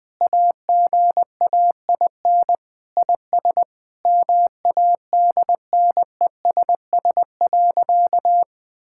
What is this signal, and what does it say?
Morse code, heard as a keyed tone, about 21 wpm, 700 Hz. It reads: AGAIN IS MADNESS.
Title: Sentences with 7-Word Phrases - 20wpm